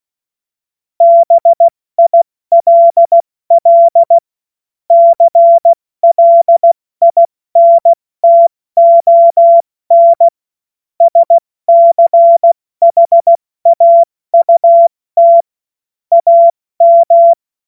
Morse code, heard as a keyed tone, about 16 wpm, 680 Hz. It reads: BILL CLINTON SCHAUT AM
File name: Morsekurs/Witze